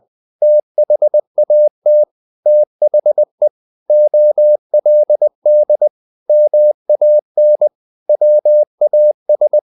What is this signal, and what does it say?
Morse code, heard as a keyed tone, about 20 wpm, 600 Hz. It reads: THAT THE OLD MAN WAS
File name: the old man and the sea part 2 20wpm